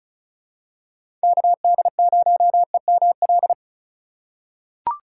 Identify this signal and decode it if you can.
Morse code, heard as a keyed tone, about 34 wpm, 700 Hz. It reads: KD0EML
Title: USA Call Signs - 2x Character Spacing - 35wpm